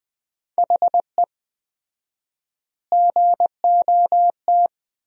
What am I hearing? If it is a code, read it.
Morse code, heard as a keyed tone, about 20 wpm, 700 Hz. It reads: HE GOT
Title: 4x Word-Spacing - Sentences from Top 400 Words - 20wpm